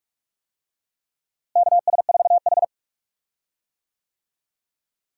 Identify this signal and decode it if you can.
Morse code, heard as a keyed tone, about 41 wpm, 700 Hz. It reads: KS4H
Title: USA Call Signs - 4 Characters - Repeated 3x - 45wpm